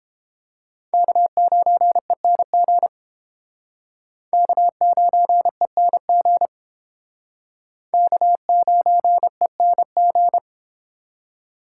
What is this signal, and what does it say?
Morse code, heard as a keyed tone, about 31 wpm, 700 Hz. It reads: K9ENG K9ENG K9ENG